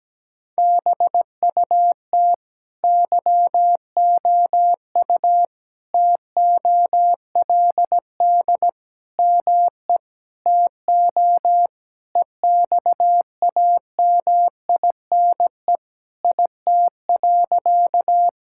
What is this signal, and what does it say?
Morse code, heard as a keyed tone, about 17 wpm, 700 Hz. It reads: BUT YOU TOLD ME TO EXAMINE IT.
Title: Sentences from Top 1600 Words 17wpm